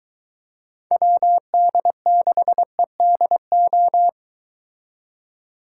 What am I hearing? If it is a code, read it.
Morse code, heard as a keyed tone, about 23 wpm, 700 Hz. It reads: WD6EDO